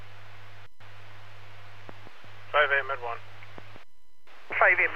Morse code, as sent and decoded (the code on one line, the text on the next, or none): none